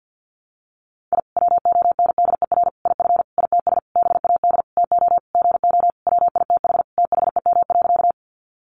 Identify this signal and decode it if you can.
Morse code, heard as a keyed tone, about 50 wpm, 700 Hz. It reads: I WONDER IF ITS BAD TO GO WITH THEM.
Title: Sentences from Top 700 Words 50wpm